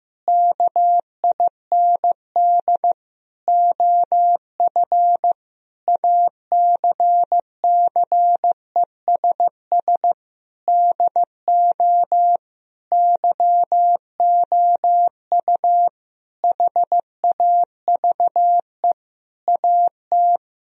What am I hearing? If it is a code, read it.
Morse code, heard as a keyed tone, about 16 wpm, 700 Hz. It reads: KIND OF ACCESS DO YOU HAVE AT
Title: Sentences from Top 1000 Words - Review Part 4 - 15wpm